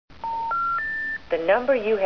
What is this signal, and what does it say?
Special information tone, with speech.